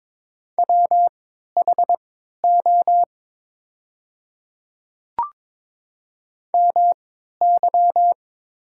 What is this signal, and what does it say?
Morse code, heard as a keyed tone, about 22 wpm, spaced as for 12 wpm, 700 Hz. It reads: WHO MY
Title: 3x Character Spacing - Top 100 Words - 22wpm